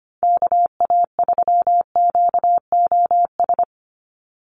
Morse code, heard as a keyed tone, about 25 wpm, 700 Hz. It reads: KA3QOS
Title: USA Call Signs 25wpm